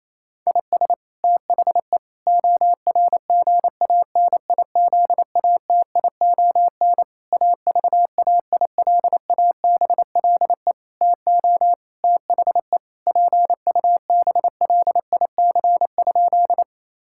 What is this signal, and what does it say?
Morse code, heard as a keyed tone, about 28 wpm, 700 Hz. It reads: IS THE ORGANIZATION AVAILABLE TO THE PUBLIC?
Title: Sentences from Top 1000 Words - Review Part 5 - 28wpm